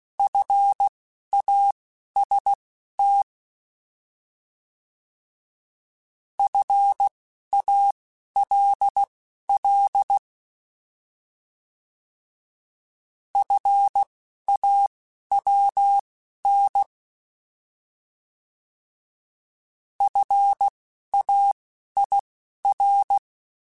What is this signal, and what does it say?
Morse code, heard as a keyed tone, about 16 wpm, spaced as for 11 wpm, 800 Hz. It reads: FAST FALL FAWN FAIR
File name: Alphabetical / Alphabetical mp3s / Fs @10